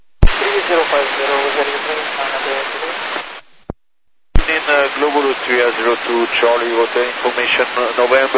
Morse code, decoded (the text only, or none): none